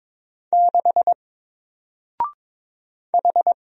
Morse code, heard as a keyed tone, about 22 wpm, 700 Hz. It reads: 6 H
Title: Single Letter-Number Alt 22wpm